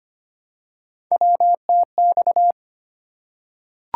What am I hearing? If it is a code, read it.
Morse code, heard as a keyed tone, about 25 wpm, 700 Hz. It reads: WTX
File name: practice / 25wpm / ARRL Sections v2 - 25wpm